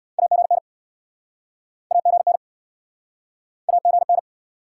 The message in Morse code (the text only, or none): AGN AGN AGN